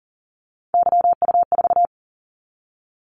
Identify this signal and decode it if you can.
Morse code, heard as a keyed tone, about 42 wpm, 700 Hz. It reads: YU4
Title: Callsign Prefixes for Top 25 DXCC - 40wpm